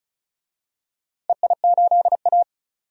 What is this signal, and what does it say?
Morse code, heard as a keyed tone, about 35 wpm, 700 Hz. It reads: EI8A
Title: DXCC-100 Call Signs - 4 Characters - 35wpm